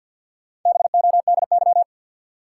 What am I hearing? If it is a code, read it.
Morse code, heard as a keyed tone, about 46 wpm, 700 Hz. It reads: BODY
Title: Sentences from Top 400 Words - New Word Drill - 50wpm